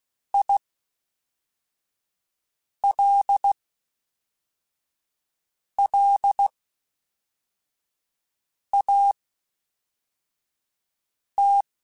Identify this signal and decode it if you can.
Morse code, heard as a keyed tone, about 16 wpm, spaced as for 4 wpm, 800 Hz. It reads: ILLAT
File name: Special INGs / Special INGs-mp3s / Fast ing @10